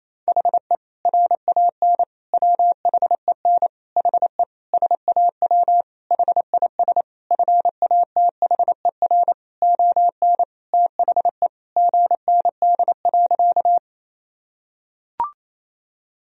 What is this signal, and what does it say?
Morse code, heard as a keyed tone, about 28 wpm, 700 Hz. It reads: HE RAN WHEN HE SAW HIS FATHER ON THE GND.
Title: Sentences from Top 500 Words and Abbreviations 28wpm